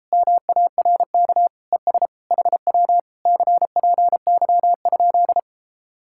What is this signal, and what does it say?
Morse code, heard as a keyed tone, about 38 wpm, 700 Hz. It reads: MARK ES HW CPY?